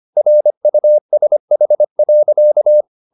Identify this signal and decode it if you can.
Morse code, heard as a keyed tone, about 25 wpm, 600 Hz. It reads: RUSH.
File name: the old man and the sea part 1 25wpm